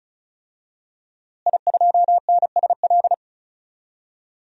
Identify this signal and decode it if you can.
Morse code, heard as a keyed tone, about 34 wpm, 700 Hz. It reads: I2NSL
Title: DXCC-100 Call Signs - 5 Characters - Repeated 2x - 35wpm